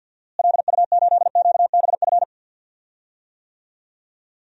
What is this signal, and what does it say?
Morse code, heard as a keyed tone, about 49 wpm, 700 Hz. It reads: LU8QBF
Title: DXCC-100 Call Signs - 6 Characters - Repeated 2x - 50wpm